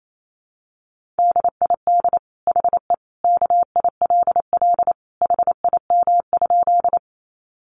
Morse code, heard as a keyed tone, about 28 wpm, 700 Hz. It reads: DID HE KILL HIM?
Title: Sentences from Top 1400 Words 28wpm